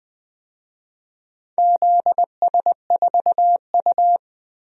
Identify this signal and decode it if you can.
Morse code, heard as a keyed tone, about 20 wpm, 700 Hz. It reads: ZS4U
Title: DXCC-100 Call Signs - 4 Characters - 20wpm